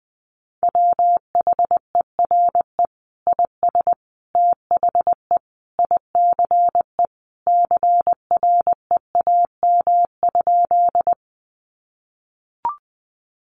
Morse code, heard as a keyed tone, about 20 wpm, 700 Hz. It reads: WHERE IS THE ICE CREAM?
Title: Sentences from Top 1500 Words 20wpm